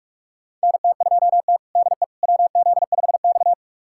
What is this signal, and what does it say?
Morse code, heard as a keyed tone, about 42 wpm, 700 Hz. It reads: NT2T DE WZ5X
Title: USA Callsign DE Callsign - 4 Characters - 45wpm